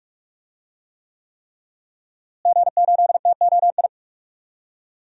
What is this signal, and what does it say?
Morse code, heard as a keyed tone, about 46 wpm, 700 Hz. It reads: G8TOI
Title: DXCC-100 Call Signs - 5 Characters - Repeated 3x - 45wpm